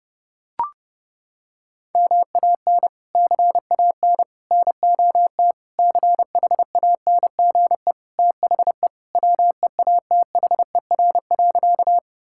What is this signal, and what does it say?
Morse code, heard as a keyed tone, about 32 wpm, 700 Hz. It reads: MAN CAN NOT CHANGE THE WEATHER.
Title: Sentences from Top 1000 Words - Review Part 5 - 30wpm